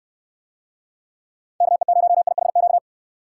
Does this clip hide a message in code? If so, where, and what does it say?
Yes: Morse code from 1.6 to 2.8 s, heard as a keyed tone, about 69 wpm, 700 Hz. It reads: KE0IHQ